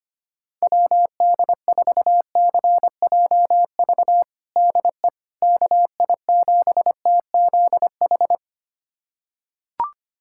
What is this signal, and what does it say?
Morse code, heard as a keyed tone, about 25 wpm, 700 Hz. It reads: WD4CJV DE KI7TZH
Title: USA Callsign DE Callsign - 6 Characters - 25wpm